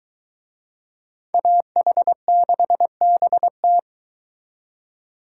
Morse code, heard as a keyed tone, about 23 wpm, 700 Hz. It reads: AH6BT